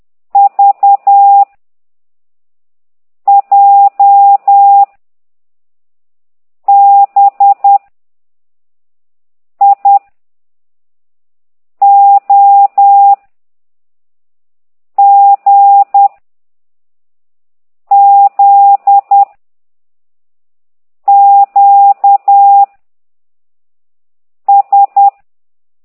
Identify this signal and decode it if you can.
Morse code, heard as a keyed tone, about 10 wpm, 800 Hz. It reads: V J B I O G Z Q S
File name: ct2idw / exerc13